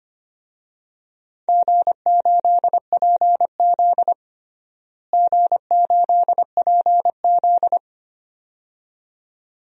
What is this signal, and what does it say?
Morse code, heard as a keyed tone, about 25 wpm, 700 Hz. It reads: G8PZ G8PZ
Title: DXCC-100 Call Signs - 4 Characters - Repeated 2x - 25wpm